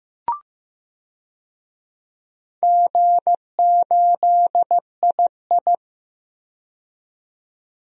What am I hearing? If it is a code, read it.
Morse code, heard as a keyed tone, about 15 wpm, 700 Hz. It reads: G8II
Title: DXCC-100 Call Signs - 4 Characters - 15wpm